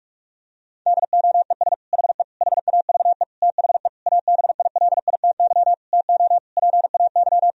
Morse code, heard as a keyed tone, about 46 wpm, 700 Hz. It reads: DOES HE HAVE THE ABILITY TO PAY
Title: Sentences from Top 1000 Words - Review Part 3 - 45wpm